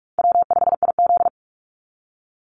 Morse code, heard as a keyed tone, about 46 wpm, 700 Hz. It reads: W5IZ
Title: Easy Call Signs - 45wpm